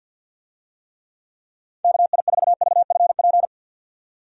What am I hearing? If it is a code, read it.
Morse code, heard as a keyed tone, about 51 wpm, 700 Hz. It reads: KI4VFP
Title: USA Call Signs - 6 Characters - 50wpm